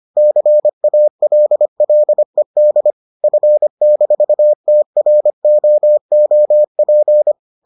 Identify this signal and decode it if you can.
Morse code, heard as a keyed tone, about 25 wpm, 600 Hz. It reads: CALLED F-TROOP